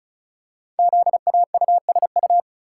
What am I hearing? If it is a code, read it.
Morse code, heard as a keyed tone, about 36 wpm, 690 Hz. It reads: ZAUSU
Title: Code Groups - 5 Letters - 35wpm